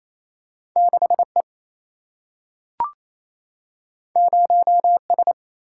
Morse code, heard as a keyed tone, about 28 wpm, 700 Hz. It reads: 6 E 0 S
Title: Single-Letter Single-Number 28wpm